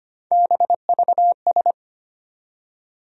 Morse code, heard as a keyed tone, about 25 wpm, 700 Hz. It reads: BVS
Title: USA Call Sign Postfixes - 3 Letters 25wpm